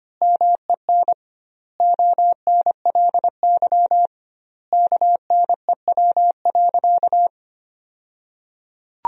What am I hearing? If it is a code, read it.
Morse code, heard as a keyed tone, about 25 wpm, 700 Hz. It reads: MEN ONLY KNEW.
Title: 2x Word-Spacing - Sentences from Top 500 Words - 25wpm